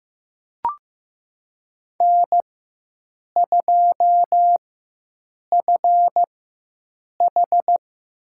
Morse code, heard as a keyed tone, about 15 wpm, spaced as for 7 wpm, 700 Hz. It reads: N2FH
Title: USA Call Signs - 5 Characters - 4x Character Spacing - 15wpm